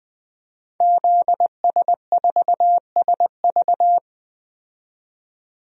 Morse code, heard as a keyed tone, about 20 wpm, 700 Hz. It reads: ZS4SV
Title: DXCC-100 Call Signs - 5 Characters - 20wpm